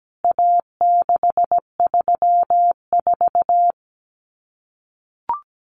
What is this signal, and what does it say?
Morse code, heard as a keyed tone, about 17 wpm, 700 Hz. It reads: A634
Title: CWT Exchange 17wpm